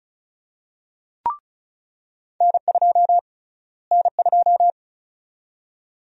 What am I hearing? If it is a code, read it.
Morse code, heard as a keyed tone, about 36 wpm, 700 Hz. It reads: N2 N2
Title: USA Call Sign Prefixes - Repeated 2x - 35wpm